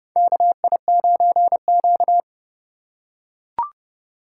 Morse code, heard as a keyed tone, about 30 wpm, 700 Hz. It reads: KI9Q
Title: USA Call Signs - 3x Character Spacing - 30wpm